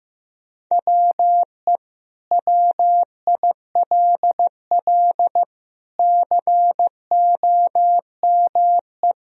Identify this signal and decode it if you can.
Morse code, heard as a keyed tone, about 16 wpm, 700 Hz. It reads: WE WILL COME
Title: Sentences from Top 1000 Words - Review Part 3 - 15wpm